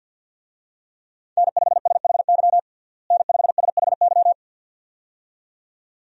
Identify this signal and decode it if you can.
Morse code, heard as a keyed tone, about 48 wpm, 700 Hz. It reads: N5SHY N5SHY
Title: USA Call Signs - 5 Characters - Repeated 2x - 50wpm